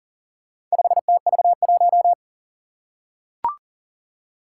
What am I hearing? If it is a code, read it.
Morse code, heard as a keyed tone, about 39 wpm, 700 Hz. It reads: 5TV1
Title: Four Challenging Letter-Number Combinations 40wpm